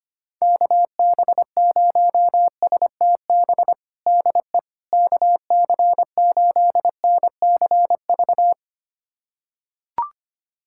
Morse code, heard as a keyed tone, about 25 wpm, 700 Hz. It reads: KB0STB DE KC8NCV
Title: USA Callsign DE Callsign - 6 Characters - 25wpm